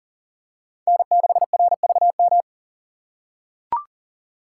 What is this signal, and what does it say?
Morse code, heard as a keyed tone, about 40 wpm, 700 Hz. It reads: N6RVM
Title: USA Call Signs - 5 Characters - 6x Character Spacing - 40wpm